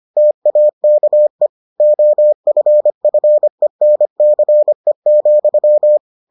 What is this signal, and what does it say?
Morse code, heard as a keyed tone, about 25 wpm, 600 Hz. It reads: TAKE OFFENCE,